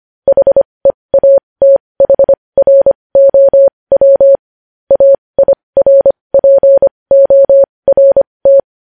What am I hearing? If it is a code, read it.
Morse code, heard as a keyed tone, about 25 wpm, spaced as for 20 wpm, 550 Hz. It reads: HEATHROW AIRPORT
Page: News Headlines in Morse Code at 20 WPM